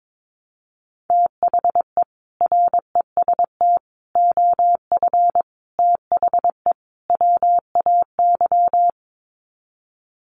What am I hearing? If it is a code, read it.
Morse code, heard as a keyed tone, about 22 wpm, 700 Hz. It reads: THE REST OF THE WAY
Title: Sets of 5 Words Encore 22wpm